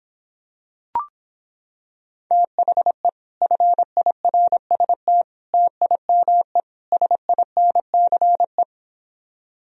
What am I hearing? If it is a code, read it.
Morse code, heard as a keyed tone, about 26 wpm, 700 Hz. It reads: THE FIRST TIME SINCE